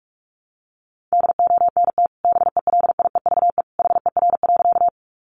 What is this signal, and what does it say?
Morse code, heard as a keyed tone, about 44 wpm, 700 Hz. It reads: DONT BELIEVE HER.
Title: Sentences from Top 1000 Words 45wpm